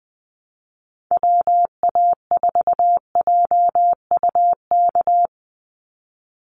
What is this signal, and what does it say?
Morse code, heard as a keyed tone, about 20 wpm, 700 Hz. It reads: WA4JUK
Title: Call Signs Repeated 3x - 20wpm